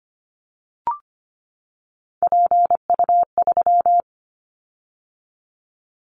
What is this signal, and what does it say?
Morse code, heard as a keyed tone, about 25 wpm, 700 Hz. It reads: PU3
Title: Callsign Prefixes for Top 100 DXCC - 25wpm